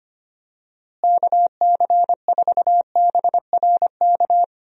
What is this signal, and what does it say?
Morse code, heard as a keyed tone, about 25 wpm, 700 Hz. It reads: KC4BRK
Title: USA Call Signs - 6 Characters - Repeated 2x - 25wpm